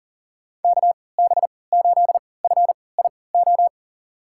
Morse code, heard as a keyed tone, about 40 wpm, spaced as for 23 wpm, 700 Hz. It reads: KB8FIO
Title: USA Call Signs - 6 Characters - 3x Character Spacing - 40wpm